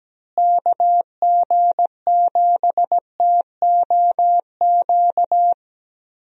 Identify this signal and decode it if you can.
Morse code, heard as a keyed tone, about 17 wpm, 700 Hz. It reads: KG7TOQ
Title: USA Call Signs - 6 Characters - Repeated 3x - 17wpm